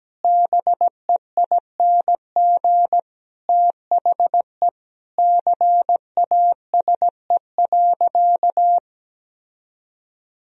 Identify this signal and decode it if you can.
Morse code, heard as a keyed tone, about 17 wpm, 700 Hz. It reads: BEING THE CASE.
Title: Sentences from Top 1000 Words - Review Part 3 - 17wpm